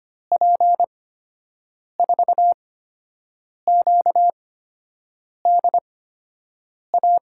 Morse code, heard as a keyed tone, about 25 wpm, spaced as for 7 wpm, 700 Hz. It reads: P4QDA
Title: USA Call Signs - 6 Characters - 8x Character Spacing - 25wpm